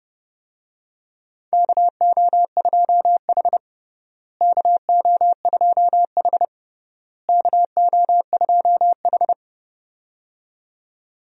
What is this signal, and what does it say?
Morse code, heard as a keyed tone, about 32 wpm, 700 Hz. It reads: KO2H KO2H KO2H
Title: USA Call Signs - 4 Characters - Repeated 3x - 30wpm